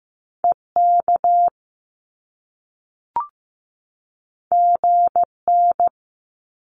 Morse code, heard as a keyed tone, about 15 wpm, 700 Hz. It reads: EK GN
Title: Top Two Letter Combinations in Words 15wpm